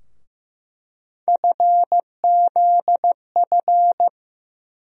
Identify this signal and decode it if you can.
Morse code, heard as a keyed tone, about 15 wpm, 700 Hz. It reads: FZF
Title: USA Call Sign Postfixes - 3 Letters 15wpm